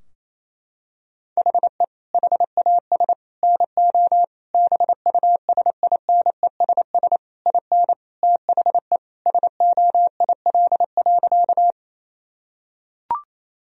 Morse code, heard as a keyed tone, about 28 wpm, 700 Hz. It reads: HE HAS NO BUSINESS IN THE SOIL.
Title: Sentences from Top 1000 Words - Review Part 4 - 28wpm